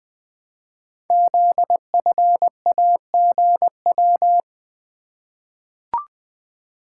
Morse code, heard as a keyed tone, about 20 wpm, 700 Hz. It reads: ZFAGW
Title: Code Groups - 5 Letters - 20wpm